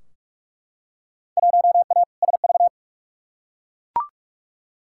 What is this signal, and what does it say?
Morse code, heard as a keyed tone, about 43 wpm, 700 Hz. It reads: 1A SV
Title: Field Day Exchanges Encore - 45wpm